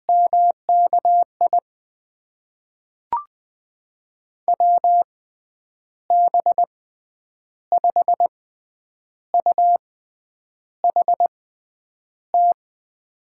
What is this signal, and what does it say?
Morse code, heard as a keyed tone, about 21 wpm, spaced as for 9 wpm, 700 Hz. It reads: MKI WB5UHT
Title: USA Call Signs - 6 Characters - 6x Character Spacing - 20wpm